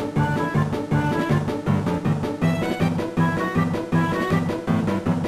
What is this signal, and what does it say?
There is no code.